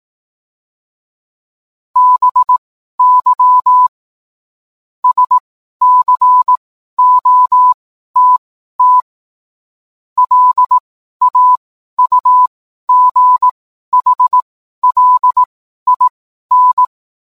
Morse code, heard as a keyed tone, about 18 wpm, spaced as for 12 wpm, 1000 Hz. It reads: BY SCOTT LAUGHLIN